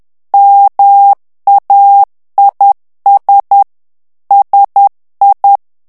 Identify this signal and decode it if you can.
Morse code, heard as a keyed tone, about 10 wpm, 800 Hz. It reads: MAIS SI
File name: morse mp3 / FRANCAIS TEXTE CLAIR 2 VITESSE 10 MOTS MINUTES